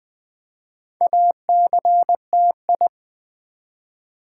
Morse code, heard as a keyed tone, about 20 wpm, 700 Hz. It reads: ACTI
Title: Build A Word - 2 Letters At A Time - 6-Letter Words - 20wpm